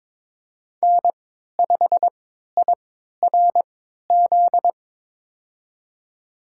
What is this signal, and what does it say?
Morse code, heard as a keyed tone, about 22 wpm, spaced as for 12 wpm, 700 Hz. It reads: N5IRZ
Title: USA Call Signs - 5 Characters - 3x Character Spacing - 22wpm